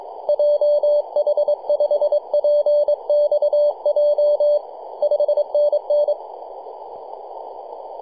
Morse code, heard as a keyed tone, about 23 wpm, 590 Hz. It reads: JH5PXJ 5NN
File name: 3d2ya-18-085-cw